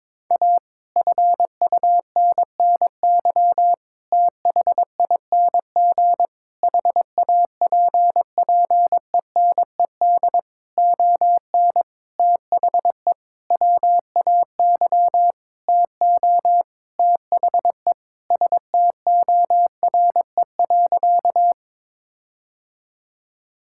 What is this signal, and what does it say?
Morse code, heard as a keyed tone, about 22 wpm, 700 Hz. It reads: A FUNNY THING HAPPENED ON THE WAY TO THE STORE.
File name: Sentences with 9-Word Phrases - 22wpm